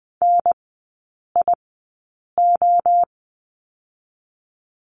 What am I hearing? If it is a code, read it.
Morse code, heard as a keyed tone, about 20 wpm, 700 Hz. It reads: N I O
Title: 029 Lesson 6e - Letter i - Three Characters 20wpm